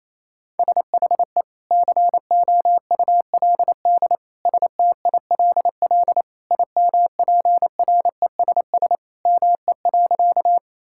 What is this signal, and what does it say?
Morse code, heard as a keyed tone, about 28 wpm, 700 Hz. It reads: SHE COULD STILL IMPRESS ME.
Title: Sentences from Top 1600 Words 28wpm